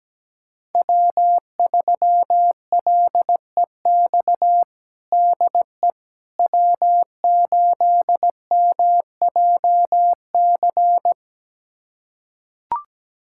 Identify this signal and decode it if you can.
Morse code, heard as a keyed tone, about 17 wpm, 700 Hz. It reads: W3LEX DE W8MJC
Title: USA Callsign DE Callsign - 5 Characters - 17wpm